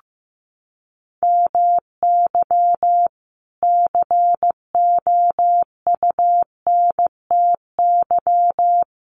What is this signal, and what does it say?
Morse code, heard as a keyed tone, about 15 wpm, 700 Hz. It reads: MY COUNTY
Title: Sentences from Top 1200 Words 15wpm